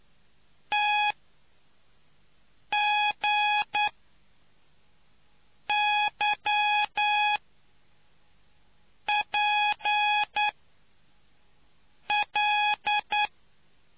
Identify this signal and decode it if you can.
Morse code, heard as a keyed tone, about 9 wpm, spaced as for 4 wpm, 840 Hz. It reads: TGYPL